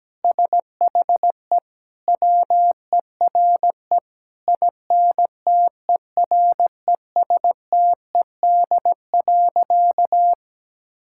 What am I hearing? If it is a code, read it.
Morse code, heard as a keyed tone, about 17 wpm, 700 Hz. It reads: SHE WERE INTERESTED.
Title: Sentences from Top 1000 Words - Review Part 6 - 17wpm